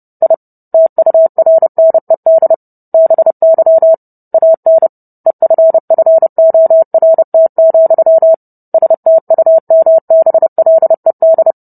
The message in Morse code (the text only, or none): I TURNED BY AN EFFORT, STUMBLED